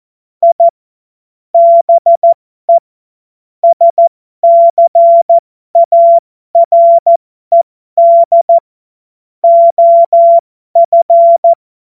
Morse code, heard as a keyed tone, about 14 wpm, spaced as for 12 wpm, 680 Hz. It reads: I BE SCARED OF